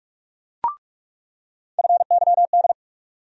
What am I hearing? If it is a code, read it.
Morse code, heard as a keyed tone, about 46 wpm, 700 Hz. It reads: FYD
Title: USA Call Sign Postfixes - 3 Letters 45wpm